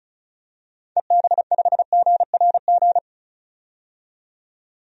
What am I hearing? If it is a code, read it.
Morse code, heard as a keyed tone, about 36 wpm, 690 Hz. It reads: EB5GRG
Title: DXCC-100 Call Signs - 6 Characters - 35wpm